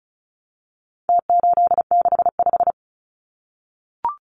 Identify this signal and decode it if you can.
Morse code, heard as a keyed tone, about 34 wpm, 700 Hz. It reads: T865
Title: Numbers - Contest Style - 4-Digit 35wpm